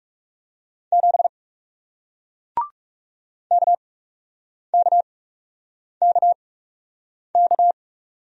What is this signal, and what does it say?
Morse code, heard as a keyed tone, about 40 wpm, spaced as for 8 wpm, 700 Hz. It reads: 7 KKKK